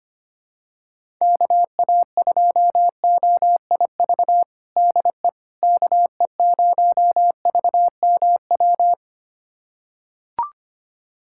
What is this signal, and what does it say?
Morse code, heard as a keyed tone, about 25 wpm, 700 Hz. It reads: KA2OIV DE KE0VMW
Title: USA Callsign DE Callsign - 6 Characters - 25wpm